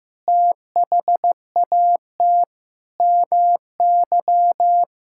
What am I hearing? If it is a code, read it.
Morse code, heard as a keyed tone, about 15 wpm, 700 Hz. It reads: THAT MY
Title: Sentences from Top 1600 Words 15wpm